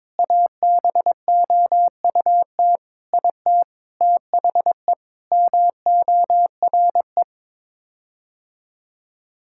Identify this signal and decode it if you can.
Morse code, heard as a keyed tone, about 22 wpm, 700 Hz. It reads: ABOUT IT THE MORE
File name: practice / 22wpm / Sets of 8 Words - 22wpm